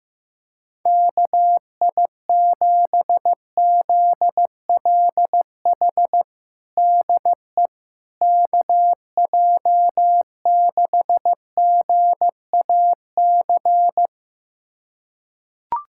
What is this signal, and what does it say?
Morse code, heard as a keyed tone, about 15 wpm, 700 Hz. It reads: KI7ZLH DE KJ6GAC
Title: USA Callsign DE Callsign - 6 Characters - 15wpm